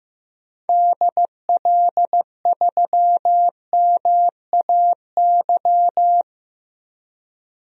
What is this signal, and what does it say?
Morse code, heard as a keyed tone, about 16 wpm, 700 Hz. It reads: DL3MAY
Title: DXCC-100 Call Signs - 6 Characters - Repeated 3x - 15wpm